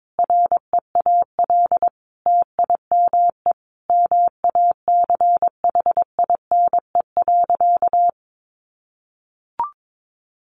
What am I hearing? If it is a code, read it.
Morse code, heard as a keyed tone, about 22 wpm, 700 Hz. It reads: REAL TIME MACHINE.
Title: A Sound Of Thunder 22wpm